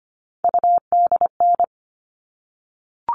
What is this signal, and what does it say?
Morse code, heard as a keyed tone, about 25 wpm, 700 Hz. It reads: UDN
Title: All Cut Numbers - 3-Digit 25wpm